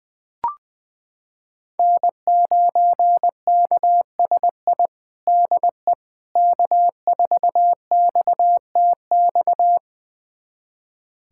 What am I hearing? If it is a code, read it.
Morse code, heard as a keyed tone, about 20 wpm, 700 Hz. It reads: N9KSI DE K4XTX
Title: USA Callsign DE Callsign - 5 Characters - 20wpm